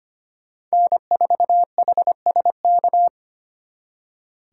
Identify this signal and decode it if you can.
Morse code, heard as a keyed tone, about 32 wpm, 700 Hz. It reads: N4HSK